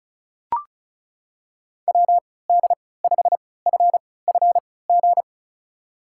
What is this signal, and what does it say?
Morse code, heard as a keyed tone, about 36 wpm, spaced as for 20 wpm, 700 Hz. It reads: WD5FFG
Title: USA Call Signs - 6 Characters - 3x Character Spacing - 35wpm